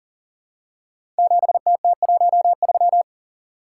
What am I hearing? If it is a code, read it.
Morse code, heard as a keyed tone, about 42 wpm, 690 Hz. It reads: 7TT13
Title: Code Groups - 5 Letter:Numbers - 40wpm